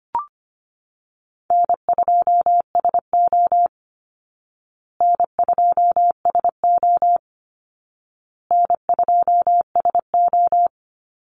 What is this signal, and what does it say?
Morse code, heard as a keyed tone, about 25 wpm, 700 Hz. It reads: N2SO N2SO N2SO
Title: Call Signs Repeated 2x - 25wpm